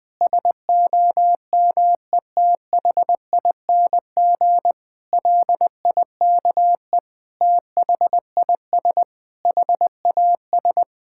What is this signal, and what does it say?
Morse code, heard as a keyed tone, about 20 wpm, 700 Hz. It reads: SOMETHING LIKE THIS HAS
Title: Sentences with 8-Word Phrases - 20wpm